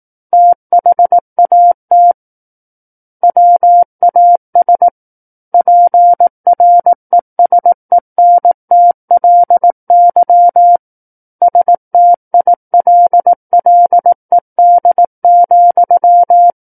Morse code, heard as a keyed tone, about 18 wpm, 700 Hz. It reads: THAT WAS PRESENTLY STILLED,
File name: Book One / 09-Chapter 9 - 18 WPM